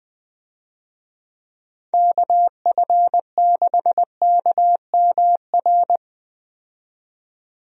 Morse code, heard as a keyed tone, about 20 wpm, 700 Hz. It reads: KF6KMR
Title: USA Call Signs - 6 Characters - 20wpm